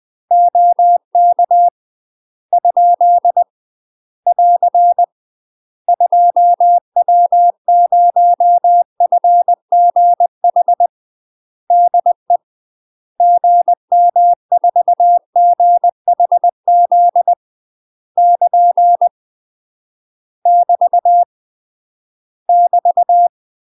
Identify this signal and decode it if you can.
Morse code, heard as a keyed tone, about 20 wpm, 700 Hz. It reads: OK ? <AR> 2W0FGH DE GM4GHZ <KN> <BT> <BT>
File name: UK MASTER QSOs TRAINING / 20WPM / UK MASTER QSOs 0041